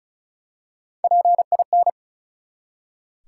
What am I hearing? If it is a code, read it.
Morse code, heard as a keyed tone, about 36 wpm, 700 Hz. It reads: PIN
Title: Three Letter Words - 35wpm